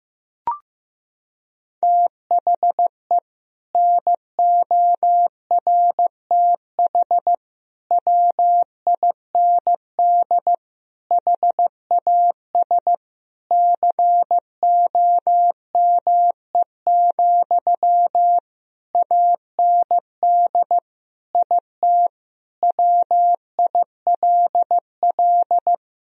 Morse code, heard as a keyed tone, about 15 wpm, 700 Hz. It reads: THE NORTH WIND HAS COME, AND IT WILL
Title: Sentences from Top 1000 Words - Review Part 3 - 15wpm